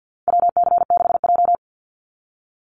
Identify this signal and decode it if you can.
Morse code, heard as a keyed tone, about 50 wpm, 700 Hz. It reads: WC6J
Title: NA SOTA Activator Callsigns 50wpm